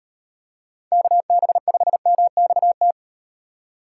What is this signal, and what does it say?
Morse code, heard as a keyed tone, about 37 wpm, 700 Hz. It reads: KB5MXT